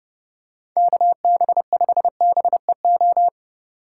Morse code, heard as a keyed tone, about 29 wpm, 700 Hz. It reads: KB5BEO